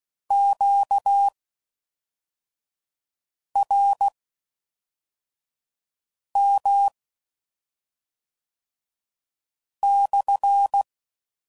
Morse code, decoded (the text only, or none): QRM/